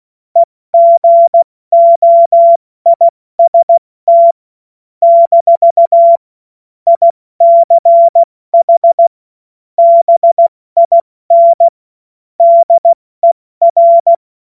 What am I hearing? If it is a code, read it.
Morse code, heard as a keyed tone, about 16 wpm, spaced as for 14 wpm, 680 Hz. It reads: EGOIST - ICH BIN DER
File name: Morsekurs / Witze / Witz 16. Februar 2021, 70 F80